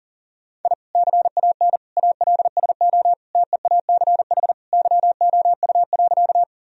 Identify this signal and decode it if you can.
Morse code, heard as a keyed tone, about 40 wpm, 700 Hz. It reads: I CAN ALSO TEACH YOU.
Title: Sentences from Top 1000 Words - Review Part 2 - 40wpm